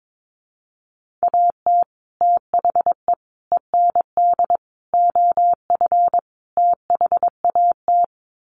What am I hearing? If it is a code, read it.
Morse code, heard as a keyed tone, about 22 wpm, 700 Hz. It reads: AT THE END OF THAT
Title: Sets of 5 Words Encore2 22wpm